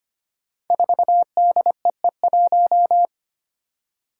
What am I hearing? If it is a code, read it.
Morse code, heard as a keyed tone, about 25 wpm, 700 Hz. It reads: 4DEE1
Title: Five Challenging Letter-Number Combinations 25wpm